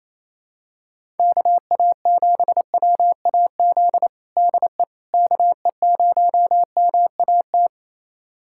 Morse code, heard as a keyed tone, about 28 wpm, 700 Hz. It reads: KA7WAZ DE KE0MAT
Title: USA Callsign DE Callsign - 6 Characters - 28wpm